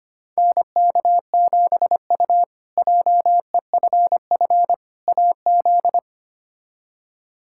Morse code, heard as a keyed tone, about 25 wpm, 700 Hz. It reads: NK7U JEFF AZ
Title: K1USN SST QSO Practice - 25wpm